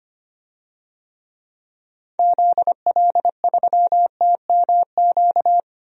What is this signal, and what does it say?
Morse code, heard as a keyed tone, about 25 wpm, 700 Hz. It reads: ZL3TMQ
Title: DXCC-100 Call Signs - 6 Characters - 25wpm